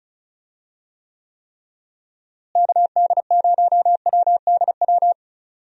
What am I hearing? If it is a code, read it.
Morse code, heard as a keyed tone, about 34 wpm, 700 Hz. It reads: KD0WDW
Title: USA Call Signs - 6 Characters - Repeated 3x - 35wpm